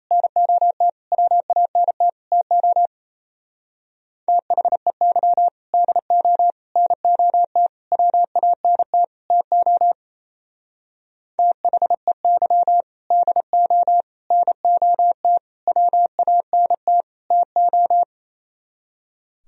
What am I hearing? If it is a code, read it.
Morse code, heard as a keyed tone, about 33 wpm, 700 Hz. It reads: NOT WANT TO THEY DO NOT WANT TO THEY DO NOT WANT TO
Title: Speed-Racing - Sets of 5 Words 28wpm